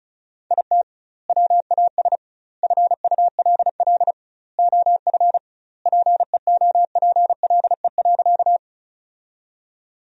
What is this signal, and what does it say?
Morse code, heard as a keyed tone, about 34 wpm, 700 Hz. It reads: IT WAS FULL OF PEOPLE.
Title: 2x Word-Spacing - Sentences from Top 500 Words - 35wpm